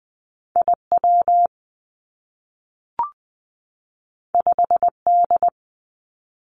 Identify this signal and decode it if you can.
Morse code, heard as a keyed tone, about 20 wpm, 700 Hz. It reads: I W 5 D
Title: Two Challenging Letter-Number Combinations 20wpm